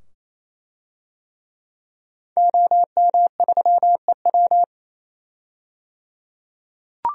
Morse code, heard as a keyed tone, about 28 wpm, 700 Hz. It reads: OM3EW
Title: DXCC-100 Call Signs - 5 Characters - 28wpm